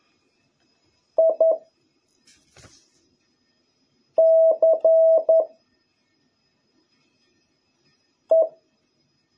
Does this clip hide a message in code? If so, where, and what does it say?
Yes: Morse code from 1.2 to 8.5 s, heard as a keyed tone, about 11 wpm, 650 Hz. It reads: I C E